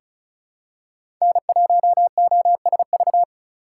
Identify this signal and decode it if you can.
Morse code, heard as a keyed tone, about 34 wpm, 690 Hz. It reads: N1OSV
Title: USA Call Signs - 5 Characters - 2x Character Spacing - 35wpm